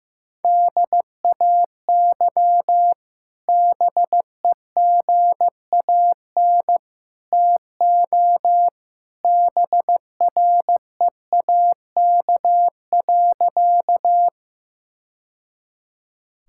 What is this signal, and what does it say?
Morse code, heard as a keyed tone, about 15 wpm, 700 Hz. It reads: DAY BEGAN TO BREAK.
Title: Sentences from Top 1000 Words - Review Part 6 - 15wpm